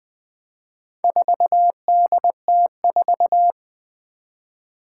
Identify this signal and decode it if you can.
Morse code, heard as a keyed tone, about 20 wpm, 700 Hz. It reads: 4DT4
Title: Four Challenging Letter-Number Combinations 20wpm